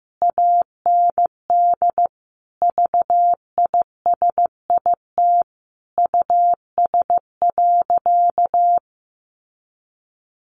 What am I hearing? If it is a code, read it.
Morse code, heard as a keyed tone, about 15 wpm, 700 Hz. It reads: AND VISIT US.
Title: Sentences from Top 1400 Words 15wpm